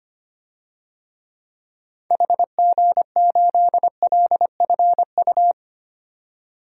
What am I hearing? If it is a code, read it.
Morse code, heard as a keyed tone, about 25 wpm, 690 Hz. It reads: HG8LFU
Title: DXCC-100 Call Signs - 6 Characters - 25wpm